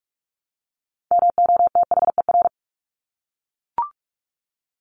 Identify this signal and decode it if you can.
Morse code, heard as a keyed tone, about 44 wpm, 700 Hz. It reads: MOTHER
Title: Top 200 Words 45wpm